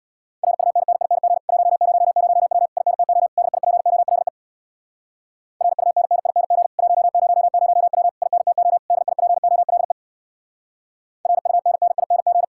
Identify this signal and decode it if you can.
Morse code, heard as a keyed tone, about 63 wpm, 700 Hz. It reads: RUNNING 100W INTO DIPOLE RUNNING 100W INTO DIPOLE RUNNING